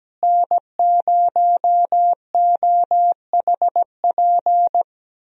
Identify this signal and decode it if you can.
Morse code, heard as a keyed tone, about 17 wpm, 700 Hz. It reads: N0OHP
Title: USA Call Signs - 5 Characters - Repeated 3x - 17wpm